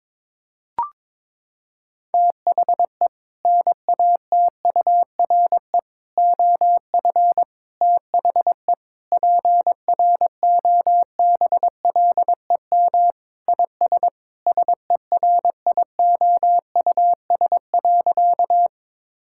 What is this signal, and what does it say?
Morse code, heard as a keyed tone, about 22 wpm, 700 Hz. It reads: THE NATURE OF THE PROBLEM IS SERIOUS.
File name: practice / 22wpm / Sentences with 3-Word Phrases - 22wpm